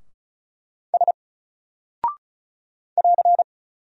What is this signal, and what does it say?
Morse code, heard as a keyed tone, about 36 wpm, 700 Hz. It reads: S <AR>
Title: Single Character (Letter:Number:Punctuation:Prosign) 35wpm